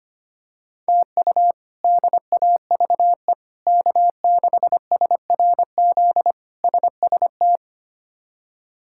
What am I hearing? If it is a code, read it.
Morse code, heard as a keyed tone, about 25 wpm, 700 Hz. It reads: TU DAVE K6SRZ SST
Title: K1USN SST QSO Practice - 25wpm